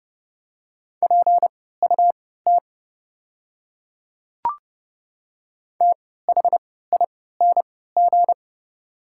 Morse code, heard as a keyed tone, about 30 wpm, spaced as for 17 wpm, 700 Hz. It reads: PUT THING
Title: 3x Character Spacing - Top 300 Words - 30wpm